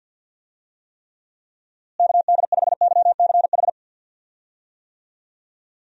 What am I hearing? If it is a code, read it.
Morse code, heard as a keyed tone, about 49 wpm, 700 Hz. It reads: KD5YCH
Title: USA Call Signs - 6 Characters - Repeated 3x - 50wpm